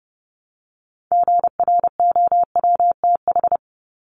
Morse code, heard as a keyed tone, about 30 wpm, 700 Hz. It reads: GROWTH
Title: Top 751-1000 Words 30wpm